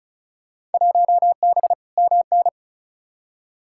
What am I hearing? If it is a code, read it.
Morse code, heard as a keyed tone, about 36 wpm, 700 Hz. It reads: 1B MN
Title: Field Day Exchanges Encore - 35wpm